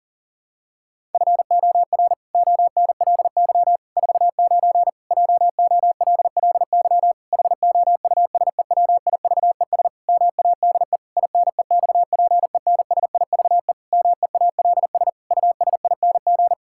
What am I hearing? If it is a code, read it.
Morse code, heard as a keyed tone, about 37 wpm, 700 Hz. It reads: FOR ONLY 49 JOLLY HOUSEWIVES MADE INEXPENSIVE MEALS USING
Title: Pangram Challenge v2 - 40wpm